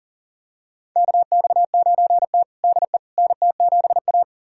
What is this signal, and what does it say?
Morse code, heard as a keyed tone, about 40 wpm, 700 Hz. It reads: KX9T DE NT7A
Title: USA Callsign DE Callsign - 4 Characters - 40wpm